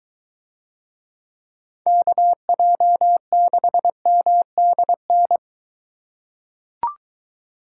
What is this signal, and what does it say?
Morse code, heard as a keyed tone, about 22 wpm, 700 Hz. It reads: KJ6MDN